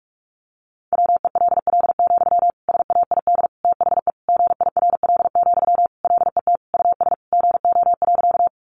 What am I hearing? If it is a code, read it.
Morse code, heard as a keyed tone, about 46 wpm, 700 Hz. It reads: WELL, SAID THE GIRL, LET US GO.
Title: Sentences from Top 400 Words 45wpm